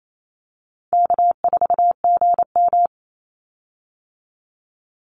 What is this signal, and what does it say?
Morse code, heard as a keyed tone, about 28 wpm, 700 Hz. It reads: K4GM
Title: Easy Call Signs Repeated 3x - 28wpm